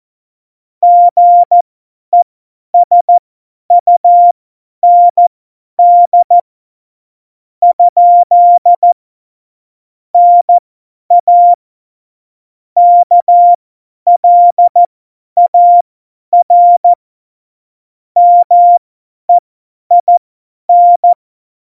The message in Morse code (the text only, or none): GESUND? NA KLAR MEIN